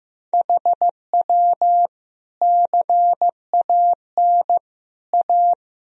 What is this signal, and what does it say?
Morse code, heard as a keyed tone, about 16 wpm, 700 Hz. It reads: HW CAN A
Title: Sentences from Top 500 Words and Abbreviations 15wpm